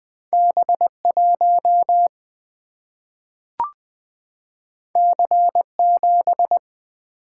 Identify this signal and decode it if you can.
Morse code, heard as a keyed tone, about 20 wpm, 700 Hz. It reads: B 1 C 7
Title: Single-Letter Single-Number 20wpm